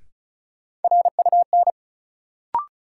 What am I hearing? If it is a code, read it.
Morse code, heard as a keyed tone, about 35 wpm, 700 Hz. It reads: RUN
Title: Sentences from Top 300 Words - New Word Drill - 35wpm